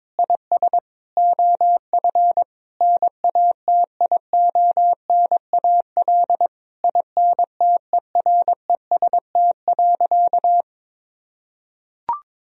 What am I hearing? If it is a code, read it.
Morse code, heard as a keyed tone, about 22 wpm, 700 Hz. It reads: IS OF NATIONAL INTEREST.
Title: Sentences from Top 1000 Words - Review Part 4 - 22wpm